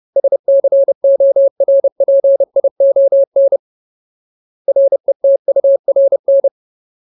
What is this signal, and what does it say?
Morse code, heard as a keyed tone, about 30 wpm, 550 Hz. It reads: SCORPION RETURN
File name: Ultimate Word List / 30WPM Ultimate